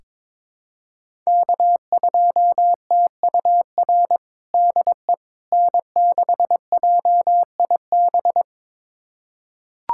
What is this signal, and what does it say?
Morse code, heard as a keyed tone, about 22 wpm, 700 Hz. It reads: K2TUR DE N6JIB
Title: USA Callsign DE Callsign - 5 Characters - 22wpm